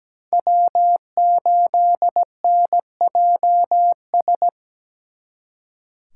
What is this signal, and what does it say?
Morse code, heard as a keyed tone, about 20 wpm, 700 Hz. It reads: W8NJS
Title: Speed Racing - USA 1x3 Call Signs 17wpm